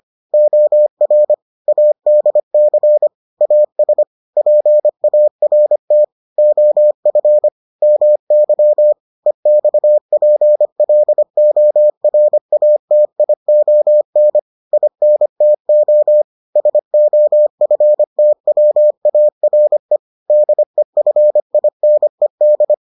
Morse code, heard as a keyed tone, about 25 wpm, 600 Hz. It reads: OR ADC AS PART OF MY EXPLORATION INTO SOFTWARE DEFINED